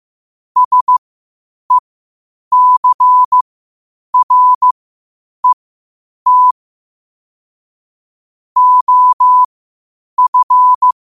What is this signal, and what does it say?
Morse code, heard as a keyed tone, about 15 wpm, spaced as for 8 wpm, 1000 Hz. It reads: SECRET OF
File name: qotdfiles / 8wpm